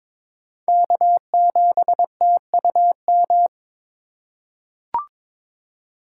Morse code, heard as a keyed tone, about 22 wpm, 700 Hz. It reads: K7TUM
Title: USA Call Signs - 5 Characters - 4x Character Spacing - 22wpm